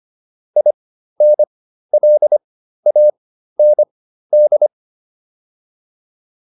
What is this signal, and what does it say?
Morse code, heard as a keyed tone, about 25 wpm, spaced as for 13 wpm, 600 Hz. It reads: INLAND